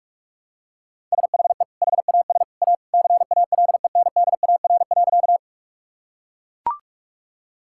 Morse code, heard as a keyed tone, about 41 wpm, 700 Hz. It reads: SHE HAS A CALENDAR.
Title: Sentences from Top 1600 Words 45wpm